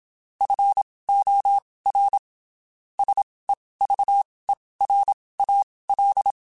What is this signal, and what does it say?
Morse code, heard as a keyed tone, about 27 wpm, spaced as for 19 wpm, 800 Hz. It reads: FOR SEVERAL
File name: (spaced) / QSOs Spaced-mp3s / kc7sac @25 sp